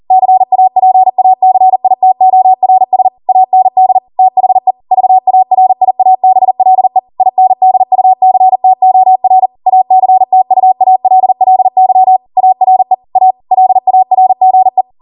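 Morse code, heard as a keyed tone, about 40 wpm, 750 Hz. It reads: CAPACITORS AND THE VARIABLE INDUCTOR ACTUALLY ARE A LARGE
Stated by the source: ARRL 40 WPM Morse code practice transmission.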